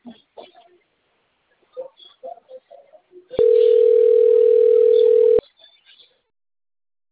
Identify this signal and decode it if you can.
Ringback tone (North America), 440 + 480 Hz.